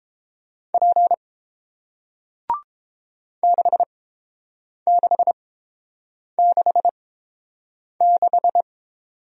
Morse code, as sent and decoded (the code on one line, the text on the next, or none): .--. / -.... -.... -.... -....
P 6666